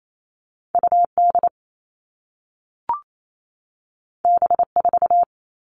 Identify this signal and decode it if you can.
Morse code, heard as a keyed tone, about 28 wpm, 700 Hz. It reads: UD B4
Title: All Cut Numbers - 2-Digit 28wpm